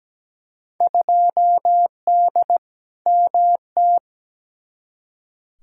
Morse code, heard as a keyed tone, about 17 wpm, 700 Hz. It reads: 2D MT
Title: Field Day Exchanges Encore - 17wpm